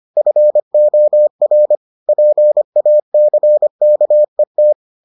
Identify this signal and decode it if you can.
Morse code, heard as a keyed tone, about 25 wpm, 600 Hz. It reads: FOR PACKET